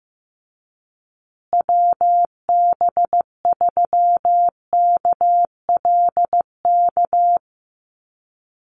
Morse code, heard as a keyed tone, about 15 wpm, 700 Hz. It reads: WB3KLK
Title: USA Call Signs Repeated 3x 15wpm